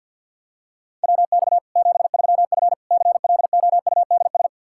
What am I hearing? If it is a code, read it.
Morse code, heard as a keyed tone, about 49 wpm, 700 Hz. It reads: WX 73F CLOUDS
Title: Practice QSO - Exchange 2 - Info - 50wpm